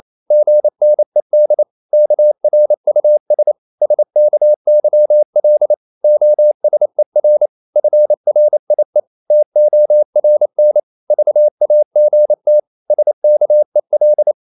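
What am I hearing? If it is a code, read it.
Morse code, heard as a keyed tone, about 28 wpm, 600 Hz. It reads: GNED KRUS SKYL OSER FRIE TORN VAGT SKEL
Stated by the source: Danish short words sent in Morse at 28 wpm, from CW med Gnister.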